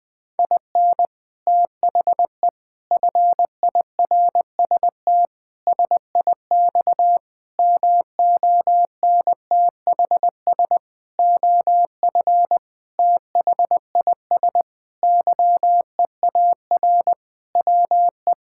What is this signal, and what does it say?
Morse code, heard as a keyed tone, about 20 wpm, 700 Hz. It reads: IN THE FIRST SIX MONTHS OF THIS YEAR WE
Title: Sentences with 8-Word Phrases - 20wpm